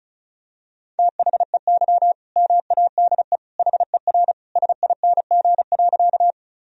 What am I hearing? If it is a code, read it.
Morse code, heard as a keyed tone, about 35 wpm, 700 Hz. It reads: THEY MADE HER SING.
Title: Sentences from Top 1000 Words - Review Part 2 - 35wpm